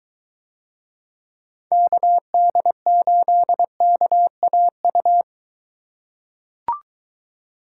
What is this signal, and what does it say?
Morse code, heard as a keyed tone, about 23 wpm, 700 Hz. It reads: KD8KAU